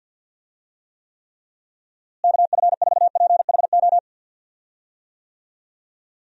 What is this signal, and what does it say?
Morse code, heard as a keyed tone, about 50 wpm, 700 Hz. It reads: KF4PHO
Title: USA Call Signs - 6 Characters - Repeated 3x - 50wpm